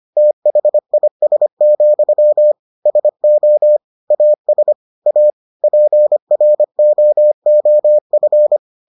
Morse code, heard as a keyed tone, about 25 wpm, 600 Hz. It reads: THIS, SO AS A PROOF